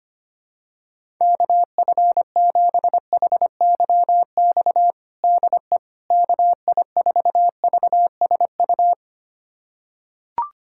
Morse code, heard as a keyed tone, about 25 wpm, 700 Hz. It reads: KF7HYX DE KI4VSU
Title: USA Callsign DE Callsign - 6 Characters - 25wpm